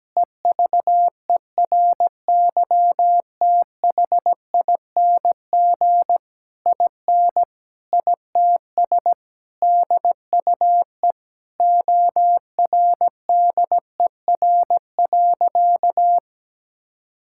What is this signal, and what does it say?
Morse code, heard as a keyed tone, about 17 wpm, 700 Hz. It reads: EVERYTHING IN ITS DUE ORDER.
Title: Sentences from Top 1000 Words - Review Part 6 - 17wpm